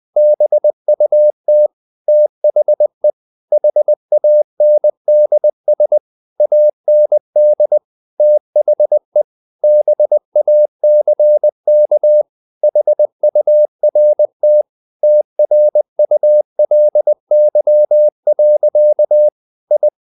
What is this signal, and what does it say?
Morse code, heard as a keyed tone, about 20 wpm, 600 Hz. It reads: BUT THE HANDS AND THE BACK HURT TRULY. I
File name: the old man and the sea part 2 20wpm